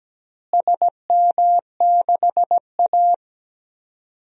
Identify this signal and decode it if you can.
Morse code, heard as a keyed tone, about 17 wpm, 700 Hz. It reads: SM6A
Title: DXCC-100 Call Signs - 4 Characters - Repeated 2x - 17wpm